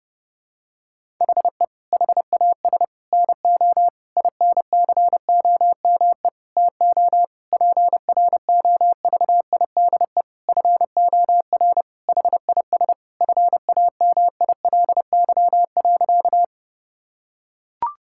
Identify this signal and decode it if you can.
Morse code, heard as a keyed tone, about 30 wpm, 700 Hz. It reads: HE HAS NO INCOME TO PROVIDE FOR HIS FAMILY.
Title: Sentences from Top 1000 Words - Review Part 5 - 30wpm